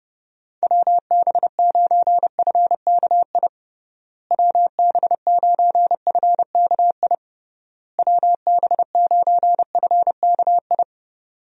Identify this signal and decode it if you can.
Morse code, heard as a keyed tone, about 30 wpm, 700 Hz. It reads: WB9FKI WB9FKI WB9FKI
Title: USA Call Signs - 6 Characters - Repeated 3x - 30wpm